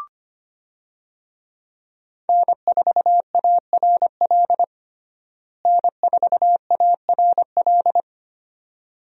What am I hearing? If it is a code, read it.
Morse code, heard as a keyed tone, about 25 wpm, 700 Hz. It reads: N4ARL N4ARL
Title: USA Call Signs - 5 Characters - Repeated 2x - 25wpm